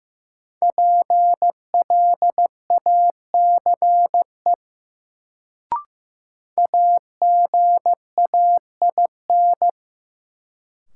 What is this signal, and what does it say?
Morse code, heard as a keyed tone, about 15 wpm, 700 Hz. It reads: PLACE AGAIN
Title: Sets of 4 Words - New Word Drill - 15wpm